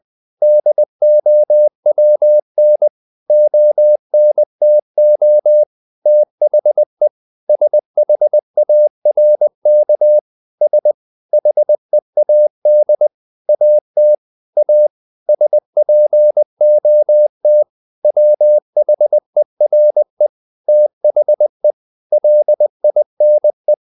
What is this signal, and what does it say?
Morse code, heard as a keyed tone, about 20 wpm, 600 Hz. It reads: DOWN ONTO THE SHARK S HEAD AT A SPOT WHERE THE LINE
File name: the old man and the sea part 2 20wpm